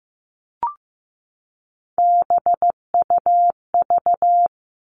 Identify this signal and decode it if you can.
Morse code, heard as a keyed tone, about 15 wpm, 700 Hz. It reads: BUV
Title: All Cut Numbers - 3-Digit 15wpm